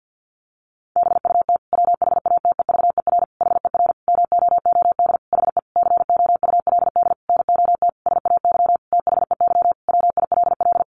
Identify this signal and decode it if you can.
Morse code, heard as a keyed tone, about 50 wpm, 700 Hz. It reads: BUT WHATEVER HER MOOD HE COULD NOT SAY THEY WILL
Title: Sentences from Top 1100 Words 50wpm